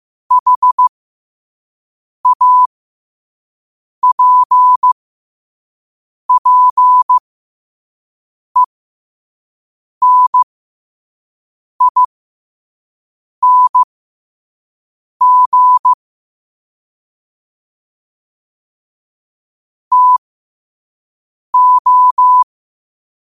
Morse code, heard as a keyed tone, about 15 wpm, spaced as for 5 wpm, 1000 Hz. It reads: HAPPENING TO